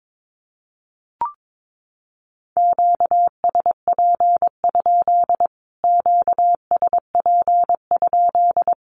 Morse code, heard as a keyed tone, about 22 wpm, 700 Hz. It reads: QSP? QSP?
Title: Q-Codes Repeated 3x 22wpm